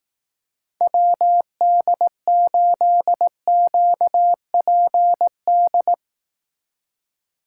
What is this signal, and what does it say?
Morse code, heard as a keyed tone, about 18 wpm, 700 Hz. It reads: WD8QPD
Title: Speed Racing - USA 2x3 Call Signs 15wpm